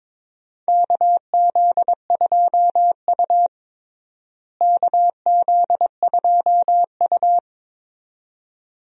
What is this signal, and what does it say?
Morse code, heard as a keyed tone, about 22 wpm, 700 Hz. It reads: KZ2U KZ2U
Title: USA Call Signs - 4 Characters - Repeated 2x - 22wpm